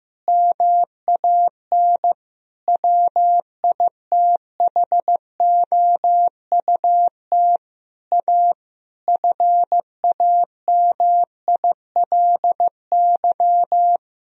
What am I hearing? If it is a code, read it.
Morse code, heard as a keyed tone, about 15 wpm, 700 Hz. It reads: MAN WITHOUT A FAMILY
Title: Sentences from Top 1000 Words - Review Part 3 - 15wpm